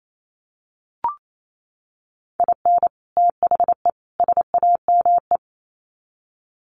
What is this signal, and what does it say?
Morse code, heard as a keyed tone, about 28 wpm, 700 Hz. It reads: IN THE SAME
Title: Sets of 3 Words 28wpm